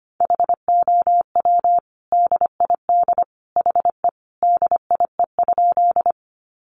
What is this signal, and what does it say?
Morse code, heard as a keyed tone, about 25 wpm, 700 Hz. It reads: HOW DID HE DIE?
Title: Sentences from Top 1300 Words 25wpm